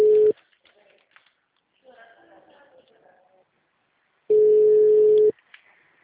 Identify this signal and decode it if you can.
Ringback tone (Europe, 425 Hz), with speech.
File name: out- -2004- - - .35